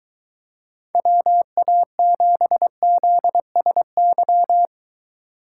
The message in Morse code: WA7ZSY

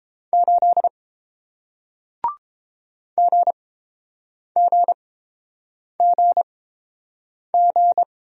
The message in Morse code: 8 GGGG

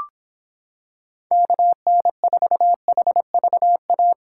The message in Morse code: KN4HVA

